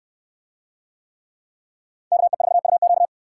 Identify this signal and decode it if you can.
Morse code, heard as a keyed tone, about 68 wpm, 700 Hz. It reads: KE4UQ